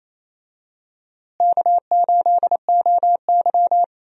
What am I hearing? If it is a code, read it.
Morse code, heard as a keyed tone, about 28 wpm, 700 Hz. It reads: K8OY
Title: USA Call Signs - 4 Characters - Repeated 3x - 28wpm